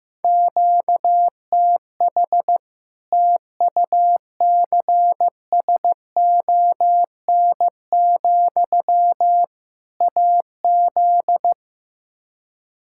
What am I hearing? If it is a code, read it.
Morse code, heard as a keyed tone, about 16 wpm, 700 Hz. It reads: QTH TUCSON, AZ
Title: Practice QSO - Exchange 1 - Beginning - 15wpm